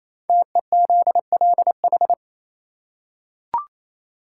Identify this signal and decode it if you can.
Morse code, heard as a keyed tone, about 28 wpm, 700 Hz. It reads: TEZLH
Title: Code Groups - 5 Letters - 28wpm